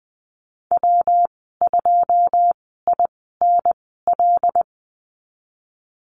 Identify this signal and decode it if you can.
Morse code, heard as a keyed tone, about 20 wpm, spaced as for 15 wpm, 700 Hz. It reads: W2INL
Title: 161 Lesson 26g - Letter g - Call Signs 20wpm